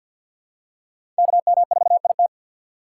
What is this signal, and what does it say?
Morse code, heard as a keyed tone, about 52 wpm, 700 Hz. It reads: KM4IT
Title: USA Call Signs - 4x Character Spacing - 50wpm